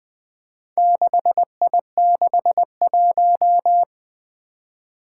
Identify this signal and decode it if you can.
Morse code, heard as a keyed tone, about 20 wpm, 700 Hz. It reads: 6I61
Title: Four Challenging Letter-Number Combinations 20wpm